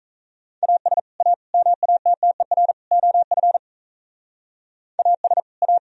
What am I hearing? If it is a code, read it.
Morse code, heard as a keyed tone, about 33 wpm, 700 Hz. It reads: AS A MATTER OF AS A
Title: Speed-Racing - Sets of 4 Words 28wpm